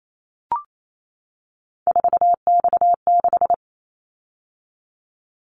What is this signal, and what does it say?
Morse code, heard as a keyed tone, about 28 wpm, 700 Hz. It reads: 4X6
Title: Callsign Prefixes for Top 50 DXCC - 28wpm